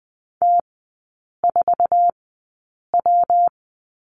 Morse code, heard as a keyed tone, about 20 wpm, spaced as for 8 wpm, 690 Hz. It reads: T4W
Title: 212 Lesson 34e - Number 8 - Three Characters 20wpm